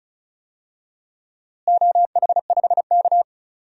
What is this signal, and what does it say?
Morse code, heard as a keyed tone, about 34 wpm, 700 Hz. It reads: OH5K
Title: DXCC-100 Call Signs - 4 Characters - Repeated 2x - 35wpm